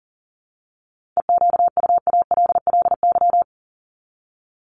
Morse code, heard as a keyed tone, about 40 wpm, 700 Hz. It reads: EQUALLY
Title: Top 1251-1500 Words 40wpm